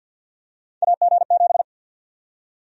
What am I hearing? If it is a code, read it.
Morse code, heard as a keyed tone, about 51 wpm, 700 Hz. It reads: AG7
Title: USA Call Sign Prefixes - 50wpm